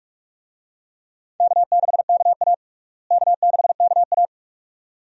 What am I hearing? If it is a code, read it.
Morse code, heard as a keyed tone, about 42 wpm, 700 Hz. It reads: K6KA K6KA
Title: USA Call Signs - 4 Characters - Repeated 2x - 45wpm